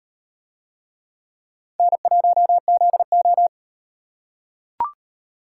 Morse code, heard as a keyed tone, about 38 wpm, 700 Hz. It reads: N1ZO